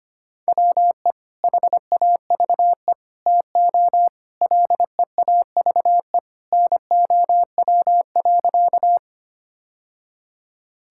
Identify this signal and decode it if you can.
Morse code, heard as a keyed tone, about 25 wpm, 700 Hz. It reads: WE HAVE TO LEAVE NOW.
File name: practice / 25wpm / Sentences with 3-Word Phrases - 25wpm